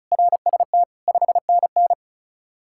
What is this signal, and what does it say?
Morse code, heard as a keyed tone, about 34 wpm, 700 Hz. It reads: RST 5NN
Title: Practice QSO - Exchange 1 - Beginning - 35wpm